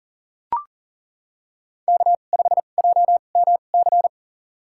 Morse code, heard as a keyed tone, about 40 wpm, spaced as for 29 wpm, 700 Hz. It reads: K5JMC
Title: USA Call Signs - 5 Characters - 2x Character Spacing - 40wpm